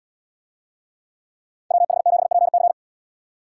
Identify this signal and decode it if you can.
Morse code, heard as a keyed tone, about 76 wpm, 700 Hz. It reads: WH7PC